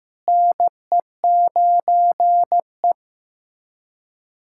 Morse code, heard as a keyed tone, about 15 wpm, 700 Hz. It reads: NE9E
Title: USA Callsign DE Callsign - 4 Characters - 15wpm